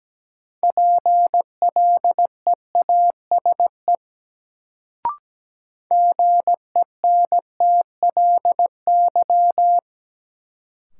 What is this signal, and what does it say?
Morse code, heard as a keyed tone, about 17 wpm, 700 Hz. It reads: PLEASE GENTLY
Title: Sentences from Top 1500 Words - New Word Drill - 17wpm